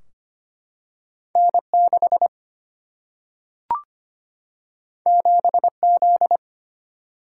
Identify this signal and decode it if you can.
Morse code, heard as a keyed tone, about 25 wpm, 700 Hz. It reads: N6 7Z
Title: Single-Letter Single-Number 25wpm